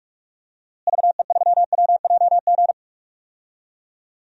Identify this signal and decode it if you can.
Morse code, heard as a keyed tone, about 47 wpm, 700 Hz. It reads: VE3WJG
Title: DXCC-100 Call Signs - 6 Characters - Repeated 2x - 45wpm